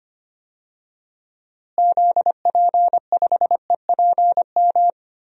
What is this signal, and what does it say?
Morse code, heard as a keyed tone, about 25 wpm, 700 Hz. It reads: ZP5EPM
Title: DXCC-100 Call Signs - 6 Characters - Repeated 2x - 25wpm